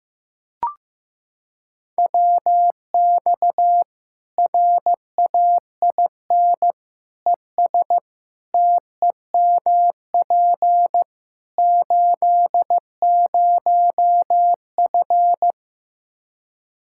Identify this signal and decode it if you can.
Morse code, heard as a keyed tone, about 15 wpm, 700 Hz. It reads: WX RAIN ES TEMP 80F
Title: Practice QSO - Exchange 2 - Info - 15wpm